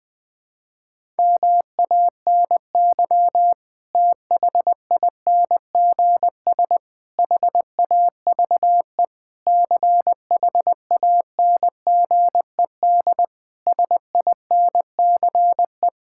Morse code, heard as a keyed tone, about 21 wpm, 700 Hz. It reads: MANY THINGS HAVE CHANGED SINCE
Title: Sentences with 7-Word Phrases - 20wpm